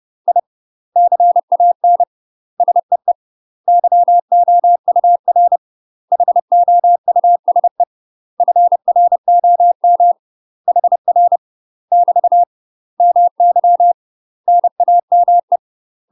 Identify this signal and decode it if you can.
Morse code, heard as a keyed tone, about 30 wpm, 700 Hz. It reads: I CAN SEE YOUR HOUSE FROM HR <BT> MY NAME
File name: UK MASTER QSOs TRAINING / 30WPM / UK MASTER QSOs 0004